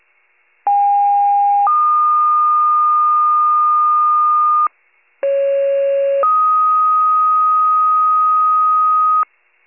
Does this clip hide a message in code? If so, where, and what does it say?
Yes: two-tone page from 0.7 to 9.2 s, heard as a one-second tone, then a three-second tone, about 810 then 1220 Hz, about 550 then 1250 Hz.